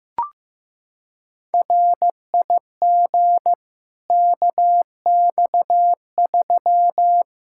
Morse code, heard as a keyed tone, about 15 wpm, 700 Hz. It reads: RIG KX3
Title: Practice QSO - Exchange 2 - Info - 15wpm